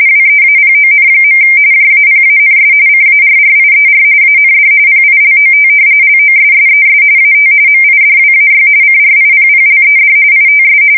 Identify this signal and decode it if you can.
RTTY, about 45.45 baud. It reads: WEAPON. JAPANS CARRIER AND PILOT LOSSES AT MIDWAY CRIPPLED ITS
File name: fsk170 / usnavy4